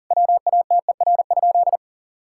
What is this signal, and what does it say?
Morse code, heard as a keyed tone, about 40 wpm, 700 Hz. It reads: WATER?
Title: Sentences from Top 1000 Words - Review Part 5 - 40wpm